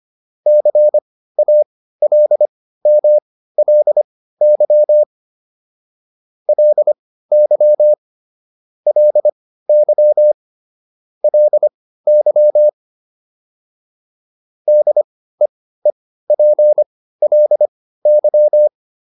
Morse code, heard as a keyed tone, about 25 wpm, spaced as for 15 wpm, 600 Hz. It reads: CALMLY LY LY LY DEEPLY